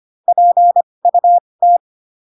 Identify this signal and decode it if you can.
Morse code, heard as a keyed tone, about 25 wpm, spaced as for 20 wpm, 700 Hz. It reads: PUT
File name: Pinocchio / Pinocchio ch6 15 25 ik1hge